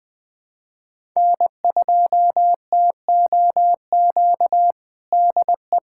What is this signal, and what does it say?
Morse code, heard as a keyed tone, about 21 wpm, 700 Hz. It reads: N2TOQ DE
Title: USA Callsign DE Callsign - 5 Characters - 20wpm